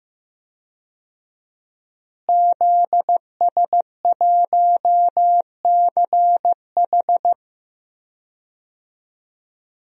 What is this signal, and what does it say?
Morse code, heard as a keyed tone, about 15 wpm, 700 Hz. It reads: ZS1CH
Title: DXCC-100 Call Signs - 5 Characters - Repeated 3x - 15wpm